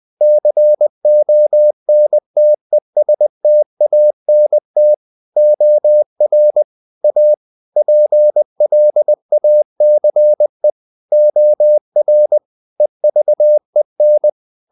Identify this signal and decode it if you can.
Morse code, heard as a keyed tone, about 20 wpm, 600 Hz. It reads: CONTESTANT OR A PLACE OR EVEN
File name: the old man and the sea / the old man and the sea part 1 20wpm